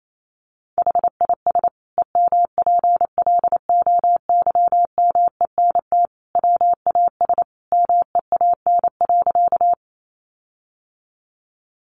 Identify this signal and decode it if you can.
Morse code, heard as a keyed tone, about 28 wpm, 700 Hz. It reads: HIS EMPLOYMENT WAS MEAN.
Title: Sentences from Top 1500 Words 28wpm